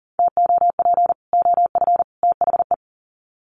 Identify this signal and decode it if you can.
Morse code, heard as a keyed tone, about 40 wpm, 700 Hz. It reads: TOP OF THE
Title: Sets of 3 Words 40wpm